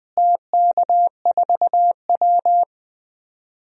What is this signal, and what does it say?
Morse code, heard as a keyed tone, about 20 wpm, 700 Hz. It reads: TK4W
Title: DXCC-100 Call Signs - 4 Characters - Repeated 2x - 20wpm